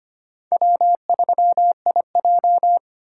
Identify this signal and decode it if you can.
Morse code, heard as a keyed tone, about 31 wpm, 700 Hz. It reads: W3IJ
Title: Speed Racing - USA Call Signs 25wpm